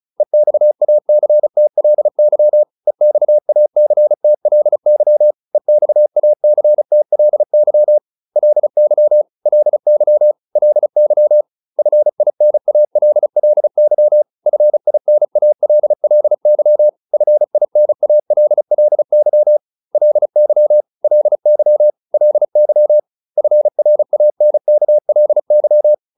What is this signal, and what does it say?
Morse code, heard as a keyed tone, about 34 wpm, 600 Hz. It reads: EXACTLY EXACTLY EXACTLY LY LY LY FINALLY FINALLY FINALLY LY LY LY FRANKLY